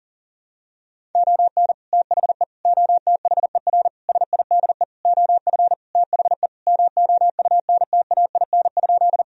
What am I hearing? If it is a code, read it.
Morse code, heard as a keyed tone, about 37 wpm, 700 Hz. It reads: ON THE OTHER SIDE OF THE MOUNTAIN?
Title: Sentences from Top 1000 Words - Review Part 1 - 40wpm